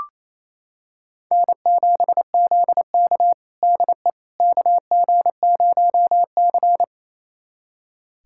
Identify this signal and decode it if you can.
Morse code, heard as a keyed tone, about 28 wpm, 700 Hz. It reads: N7ZK DE KG0C
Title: USA Callsign DE Callsign - 4 Characters - 28wpm